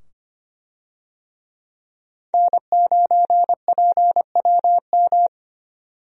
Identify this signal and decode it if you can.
Morse code, heard as a keyed tone, about 25 wpm, 700 Hz. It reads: N9PWM
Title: USA Call Signs - 5 Characters - 25wpm